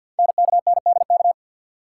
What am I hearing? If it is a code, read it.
Morse code, heard as a keyed tone, about 50 wpm, 700 Hz. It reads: NKNDK
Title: Code Groups - 5 Letters - 50wpm